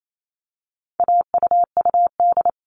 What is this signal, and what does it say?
Morse code, heard as a keyed tone, about 28 wpm, 700 Hz. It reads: AUUD
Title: All Cut Numbers - 4-Digit 28wpm